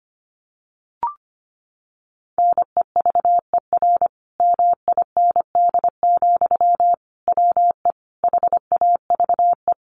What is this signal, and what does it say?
Morse code, heard as a keyed tone, about 25 wpm, 700 Hz. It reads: NEVER MIND, WE HAVE
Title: Sentences from Top 1300 Words 25wpm